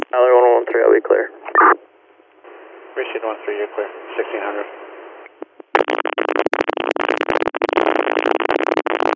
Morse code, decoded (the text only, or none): none